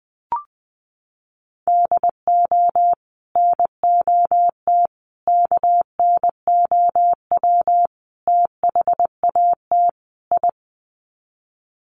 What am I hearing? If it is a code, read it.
Morse code, heard as a keyed tone, about 20 wpm, 700 Hz. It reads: DO NOT KNOW THAT I
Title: Sets of 5 Words Encore2 20wpm